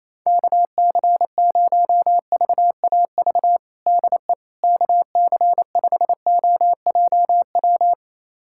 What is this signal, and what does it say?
Morse code, heard as a keyed tone, about 28 wpm, 700 Hz. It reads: KC0VAV DE KC5OJW
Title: USA Callsign DE Callsign 28wpm